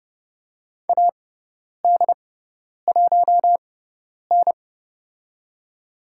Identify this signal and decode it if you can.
Morse code, heard as a keyed tone, about 30 wpm, spaced as for 10 wpm, 700 Hz. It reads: AD1N
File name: Lesson 18g - Letter m - Call Signs 30wpm